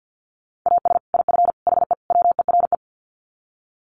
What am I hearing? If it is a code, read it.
Morse code, heard as a keyed tone, about 50 wpm, 700 Hz. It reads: AS IF HE WERE
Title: Sets of 4 Words 50wpm